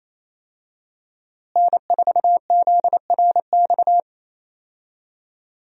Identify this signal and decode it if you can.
Morse code, heard as a keyed tone, about 27 wpm, 700 Hz. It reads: N4ZRX
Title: USA Call Signs - 5 Characters - Repeated 3x - 28wpm